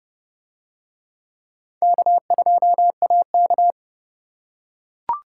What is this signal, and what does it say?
Morse code, heard as a keyed tone, about 29 wpm, 700 Hz. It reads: K2AK